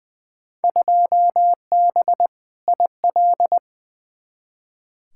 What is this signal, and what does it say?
Morse code, heard as a keyed tone, about 20 wpm, 700 Hz. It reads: 2B IL
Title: Field Day Exchanges Encore - 20wpm